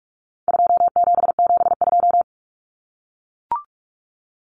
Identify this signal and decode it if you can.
Morse code, heard as a keyed tone, about 44 wpm, 700 Hz. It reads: 2772
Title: Common Cut Numbers - 4-Digit 45wpm